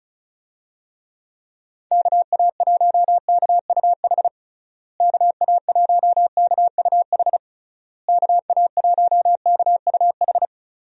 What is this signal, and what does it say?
Morse code, heard as a keyed tone, about 34 wpm, 700 Hz. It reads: KA1KUH KA1KUH KA1KUH
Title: USA Call Signs - 6 Characters - Repeated 3x - 35wpm